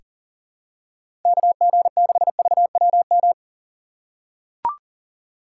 Morse code, heard as a keyed tone, about 40 wpm, 700 Hz. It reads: KG6VWM
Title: USA Call Signs - 3x Character Spacing - 40wpm